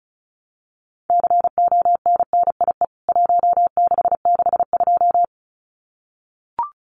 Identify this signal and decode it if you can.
Morse code, heard as a keyed tone, about 34 wpm, 700 Hz. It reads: CONNIE 1662
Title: CWT Exchange 35wpm